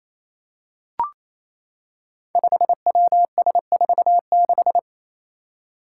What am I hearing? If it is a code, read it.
Morse code, heard as a keyed tone, about 28 wpm, 700 Hz. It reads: 5WS46
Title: Five Challenging Letter-Number Combinations 28wpm